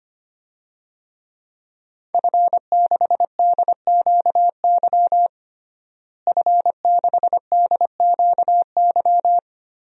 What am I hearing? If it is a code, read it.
Morse code, heard as a keyed tone, about 25 wpm, 700 Hz. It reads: F6DQY F6DQY
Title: DXCC-100 Call Signs - 5 Characters - Repeated 3x - 25wpm